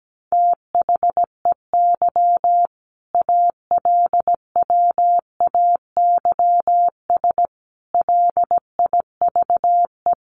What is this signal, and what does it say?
Morse code, heard as a keyed tone, about 17 wpm, 700 Hz. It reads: THEY ALWAYS LIVE
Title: Sentences from Top 1400 Words 17wpm